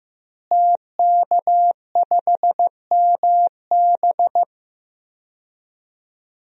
Morse code, heard as a keyed tone, about 16 wpm, 700 Hz. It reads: TK5MB
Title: DXCC-100 Call Signs - 5 Characters - Repeated 3x - 15wpm